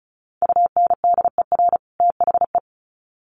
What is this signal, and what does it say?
Morse code, heard as a keyed tone, about 34 wpm, 700 Hz. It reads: UNDER THE
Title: Sets of 2 Words 35wpm